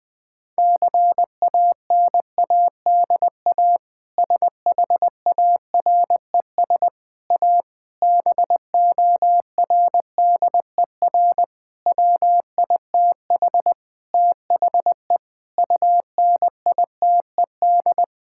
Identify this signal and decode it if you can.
Morse code, heard as a keyed tone, about 21 wpm, 700 Hz. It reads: CANADA SHARES A BORDER WITH THE UNITED
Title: Sentences with 4-Word Phrases - 20wpm